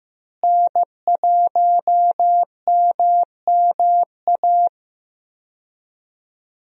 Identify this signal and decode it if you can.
Morse code, heard as a keyed tone, about 15 wpm, 700 Hz. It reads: N1MMA
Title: USA Call Signs - 5 Characters - Repeated 2x - 15wpm